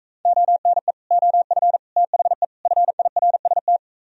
Morse code, heard as a keyed tone, about 38 wpm, 700 Hz. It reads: ONE OF THE FIRST